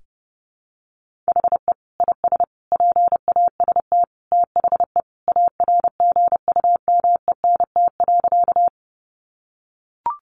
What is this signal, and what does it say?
Morse code, heard as a keyed tone, about 30 wpm, 700 Hz. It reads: HE IS PAST THE ARGUMENT.
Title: Sentences from Top 1400 Words 30wpm